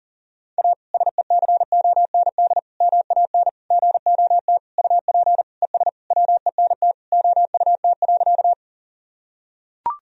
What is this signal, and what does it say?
Morse code, heard as a keyed tone, about 40 wpm, 700 Hz. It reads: A SECOND MAN GOT UP ES WENT OUT.
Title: Sentences from Top 300 Words and Abbreviations 40wpm